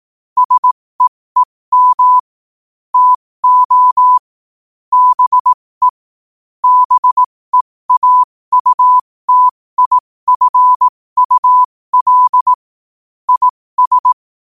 Morse code, heard as a keyed tone, about 18 wpm, spaced as for 15 wpm, 1000 Hz. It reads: SEEM TO BE BEAUTIFUL IS
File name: qotdfiles / 15wpm